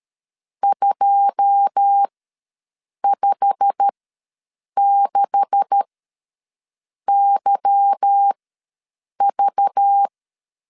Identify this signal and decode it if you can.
Morse code, heard as a keyed tone, about 13 wpm, spaced as for 6 wpm, 800 Hz. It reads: 256YV
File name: LEZIONI / 9 286 ws 12wpm 3351